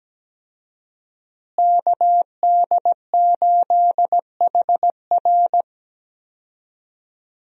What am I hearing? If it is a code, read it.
Morse code, heard as a keyed tone, about 17 wpm, 700 Hz. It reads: KD8HR
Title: USA Call Signs - 5 Characters - Repeated 3x - 17wpm